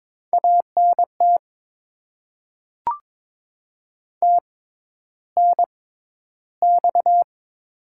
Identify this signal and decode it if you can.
Morse code, heard as a keyed tone, about 22 wpm, spaced as for 11 wpm, 700 Hz. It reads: ANT TNX
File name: Top 20 QSO Elements - 6x Character Spacing - 22wpm